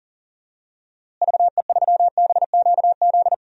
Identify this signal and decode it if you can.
Morse code, heard as a keyed tone, about 40 wpm, 700 Hz. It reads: VE3BQZ
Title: DXCC-100 Call Signs - 6 Characters - Repeated 3x - 40wpm